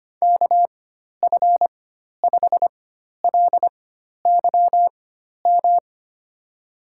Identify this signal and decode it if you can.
Morse code, heard as a keyed tone, about 25 wpm, spaced as for 12 wpm, 700 Hz. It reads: KF5LYM
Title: USA Call Signs - 6 Characters - 4x Character Spacing - 25wpm